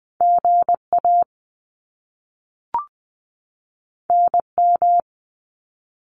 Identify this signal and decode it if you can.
Morse code, heard as a keyed tone, about 20 wpm, 700 Hz. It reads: GA NM
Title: US States 20wpm